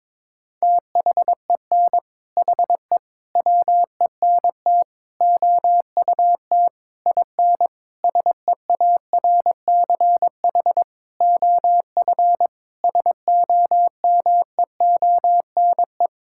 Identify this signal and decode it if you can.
Morse code, heard as a keyed tone, about 22 wpm, 700 Hz. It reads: THEN HE WENT OUT IN SEARCH OF SOMEONE
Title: Sentences from Top 1000 Words - Review Part 5 - 22wpm